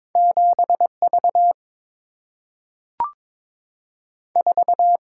Morse code, heard as a keyed tone, about 22 wpm, 690 Hz. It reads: 7 V 4
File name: practice / 22wpm / Two Letter-Number Combinations - 22wpm